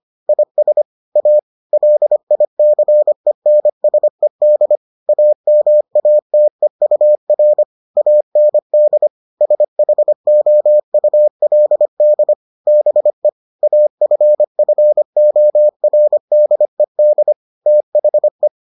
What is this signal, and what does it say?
Morse code, heard as a keyed tone, about 25 wpm, 600 Hz. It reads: IS A LICENSED AMATEUR AND SHOULD BE AFFORDED THE